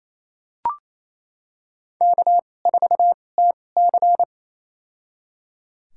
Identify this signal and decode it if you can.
Morse code, heard as a keyed tone, about 28 wpm, spaced as for 20 wpm, 700 Hz. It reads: K4TC
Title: USA Call Signs - 4 Characters - 2x Character Spacing - 28wpm